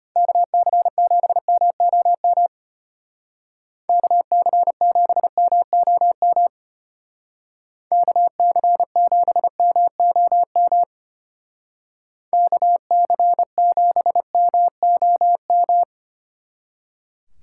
Morse code, heard as a keyed tone, about 31 wpm, 700 Hz. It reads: KC7MOM KC7MOM KC7MOM KC7MOM
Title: Speed Racing - USA 2x3 Call Signs 25wpm